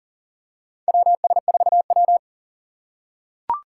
Morse code, heard as a keyed tone, about 40 wpm, 700 Hz. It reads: WS4W
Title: USA Call Signs - 3x Character Spacing - 40wpm